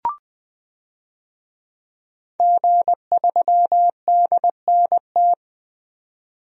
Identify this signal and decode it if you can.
Morse code, heard as a keyed tone, about 20 wpm, 700 Hz. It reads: G3DNT
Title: DXCC-100 Call Signs - 5 Characters - Repeated 3x - 20wpm